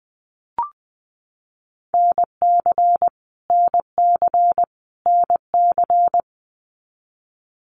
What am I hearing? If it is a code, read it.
Morse code, heard as a keyed tone, about 20 wpm, 700 Hz. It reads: NC NC NC
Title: US States Repeated 3x 20wpm